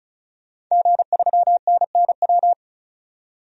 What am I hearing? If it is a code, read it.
Morse code, heard as a keyed tone, about 34 wpm, 700 Hz. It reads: G3NNW
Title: DXCC-100 Call Signs - 5 Characters - 35wpm